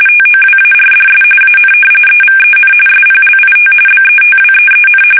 RTTY, about 45.45 baud. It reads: TOWARDS AN EVEN SPLIT BETWEEN
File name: fsk850 / usnavy5